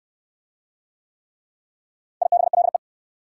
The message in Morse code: IBLE

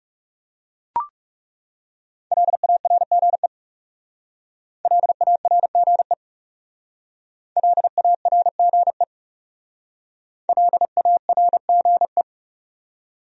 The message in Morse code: LARGE LARGE LARGE LARGE

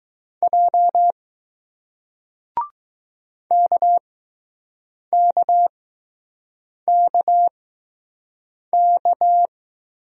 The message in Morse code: J KKKK

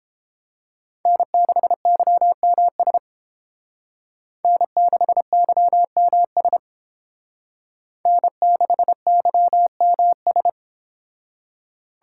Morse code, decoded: N6YMS N6YMS N6YMS

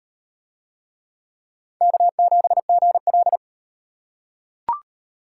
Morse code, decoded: K7GL

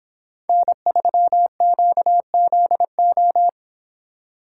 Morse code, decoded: N3QZO

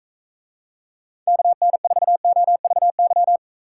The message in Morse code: KN4OVY

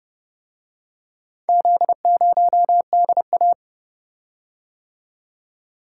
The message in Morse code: Z0DA

